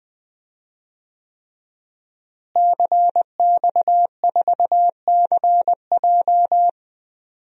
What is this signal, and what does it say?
Morse code, heard as a keyed tone, about 20 wpm, 700 Hz. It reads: CX4CJ
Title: DXCC-100 Call Signs - 5 Characters - Repeated 3x - 20wpm